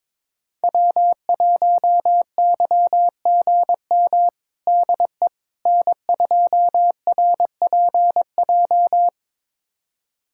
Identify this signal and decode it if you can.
Morse code, heard as a keyed tone, about 22 wpm, 700 Hz. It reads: W1YGM DE N2RPJ
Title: USA Callsign DE Callsign - 5 Characters - 22wpm